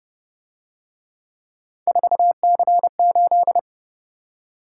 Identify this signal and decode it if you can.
Morse code, heard as a keyed tone, about 31 wpm, 700 Hz. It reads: 4C8
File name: Speed-Racing - Three Letter-Number Combinations - 25wpm